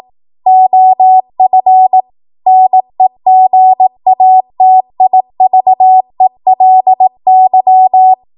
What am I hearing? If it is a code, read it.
Morse code, heard as a keyed tone, about 18 wpm, 750 Hz. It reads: OF NEGATIVELY
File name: Morse / 18WPM